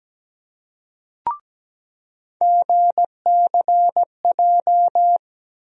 Morse code, heard as a keyed tone, about 17 wpm, 700 Hz. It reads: GCJ
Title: USA Call Sign Postfixes - 3 Letters 17wpm